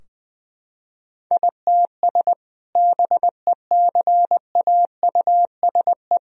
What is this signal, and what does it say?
Morse code, heard as a keyed tone, about 20 wpm, 700 Hz. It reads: ITS BECAUSE
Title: Sentences from Top 1000 Words - Review Part 3 - 20wpm